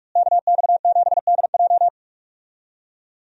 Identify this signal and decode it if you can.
Morse code, heard as a keyed tone, about 46 wpm, 700 Hz. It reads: KX7DJ
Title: USA Call Signs - 5 Characters - Repeated 3x - 45wpm